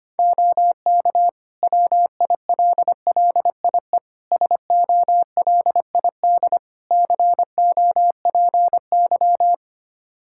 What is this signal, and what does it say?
Morse code, heard as a keyed tone, about 25 wpm, 700 Hz. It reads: OK WILLIE SOLID COPY
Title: Practice QSO - Exchange 3 - More Info - 25wpm